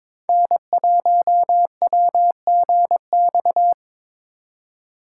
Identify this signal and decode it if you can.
Morse code, heard as a keyed tone, about 24 wpm, 700 Hz. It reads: N1WGX
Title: Speed Racing - USA 1x3 Call Signs 22wpm